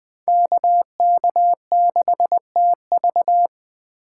Morse code, heard as a keyed tone, about 21 wpm, 700 Hz. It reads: KK6TV
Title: K1USN SST QSO Practice - 20wpm